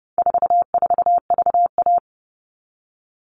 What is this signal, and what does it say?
Morse code, heard as a keyed tone, about 30 wpm, 700 Hz. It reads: 44VA
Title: All Cut Numbers - 4-Digit 30wpm